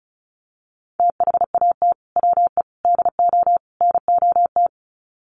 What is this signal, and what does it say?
Morse code, heard as a keyed tone, about 34 wpm, 700 Hz. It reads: THAT WE DO NOT
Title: Sets of 4 Words 35wpm